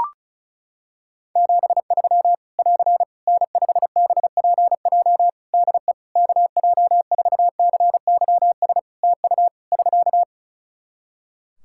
Morse code, heard as a keyed tone, about 34 wpm, 700 Hz. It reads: 73 <AR> N5BPJ DE KJ4CYS TU <SK>
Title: Practice QSO - Exchange 4 - Ending - 35wpm